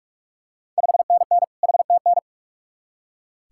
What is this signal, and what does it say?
Morse code, heard as a keyed tone, about 42 wpm, 700 Hz. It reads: 5NN HTN
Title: Ontario QSO Party Exchange 45wpm